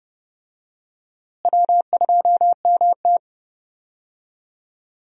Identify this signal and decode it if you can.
Morse code, heard as a keyed tone, about 30 wpm, 700 Hz. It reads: W2MT
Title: USA Call Signs - 4 Characters - Repeated 2x - 30wpm